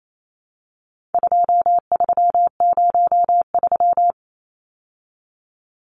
Morse code, heard as a keyed tone, about 28 wpm, 700 Hz. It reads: 2303
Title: Numbers - 4-Digit 28wpm